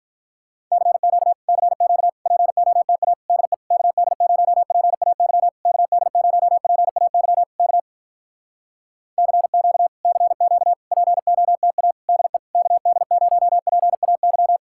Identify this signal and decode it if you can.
Morse code, heard as a keyed tone, about 46 wpm, 700 Hz. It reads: CQ CQ POTA DE KD0PAY KD0PAY K CQ CQ POTA DE KD0PAY